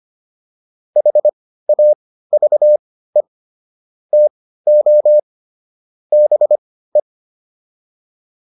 Morse code, heard as a keyed tone, about 25 wpm, spaced as for 15 wpm, 600 Hz. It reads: HAVE TO BE